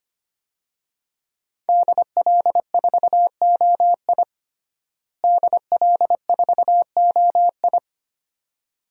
Morse code, heard as a keyed tone, about 25 wpm, 700 Hz. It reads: DL4OI DL4OI
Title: USA Call Signs - 5 Characters - Repeated 2x - 25wpm